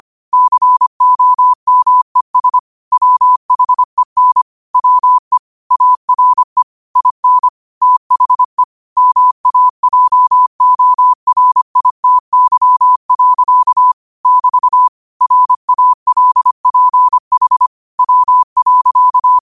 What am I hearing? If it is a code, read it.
Morse code, heard as a keyed tone, about 25 wpm, 1000 Hz. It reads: COMES WHEN WE ARE IN THE MAJORITY. <BT> RALPH W.
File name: qotdfiles / 25wpm